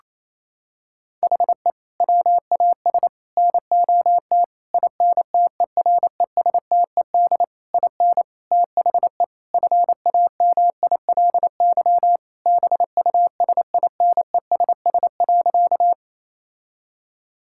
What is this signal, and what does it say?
Morse code, heard as a keyed tone, about 27 wpm, 700 Hz. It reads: HE WAS NOT INTERESTED IN THE FAMILY BUSINESS.
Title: Sentences from Top 1000 Words - Review Part 6 - 28wpm